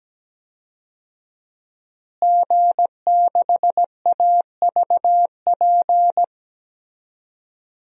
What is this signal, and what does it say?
Morse code, heard as a keyed tone, about 17 wpm, 700 Hz. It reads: G6AVP
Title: DXCC-100 Call Signs - 5 Characters - Repeated 3x - 17wpm